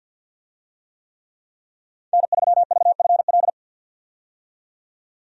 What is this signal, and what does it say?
Morse code, heard as a keyed tone, about 51 wpm, 700 Hz. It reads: N3VFL
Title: USA Call Signs - 5 Characters - 50wpm